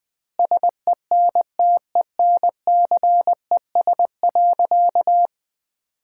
Morse code, heard as a keyed tone, about 20 wpm, 700 Hz. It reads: SENTENCES.
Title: Pangram Challenge v2 - 20wpm